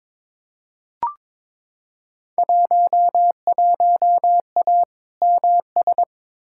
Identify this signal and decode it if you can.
Morse code, heard as a keyed tone, about 22 wpm, 700 Hz. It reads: 11A MS
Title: Field Day Exchanges Encore - 22wpm